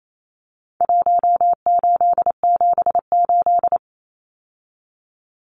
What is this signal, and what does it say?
Morse code, heard as a keyed tone, about 28 wpm, 700 Hz. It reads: 1878
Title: Numbers - Contest Style - 4-Digit 28wpm